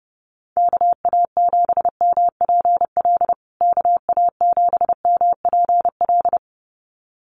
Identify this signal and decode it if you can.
Morse code, heard as a keyed tone, about 30 wpm, 700 Hz. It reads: KA7MPL KA7MPL
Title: USA Call Signs Repeated 2x 30wpm